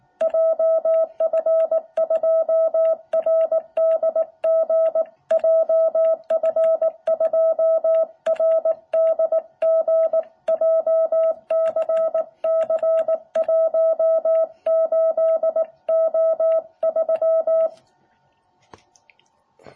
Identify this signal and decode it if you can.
Morse code, heard as a keyed tone, about 18 wpm, 660 Hz. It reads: JF2RDG JF2RDG JCC18O3
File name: jh8cbh / 2021cw